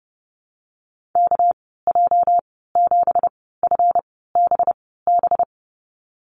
Morse code, heard as a keyed tone, about 31 wpm, spaced as for 17 wpm, 700 Hz. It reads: KJ7FBB
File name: Lesson 37g - Letter x - Call Signs 30wpm